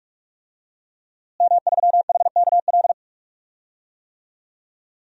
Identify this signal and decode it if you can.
Morse code, heard as a keyed tone, about 42 wpm, 700 Hz. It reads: M3HKL
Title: DXCC-100 Call Signs - 5 Characters - Repeated 2x - 45wpm